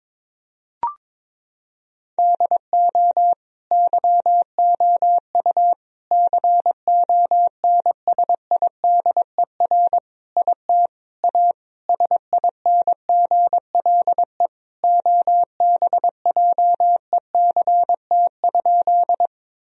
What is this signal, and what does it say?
Morse code, heard as a keyed tone, about 22 wpm, 700 Hz. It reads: DO YOU CONSIDER IT A SINGLE OBJECT?
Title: Sentences from Top 1000 Words - Review Part 4 - 22wpm